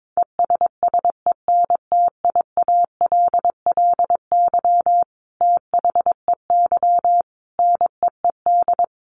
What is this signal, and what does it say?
Morse code, heard as a keyed tone, about 22 wpm, 700 Hz. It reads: ESSENTIALLY THEY NEED
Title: Sentences from Top 1300 Words 22wpm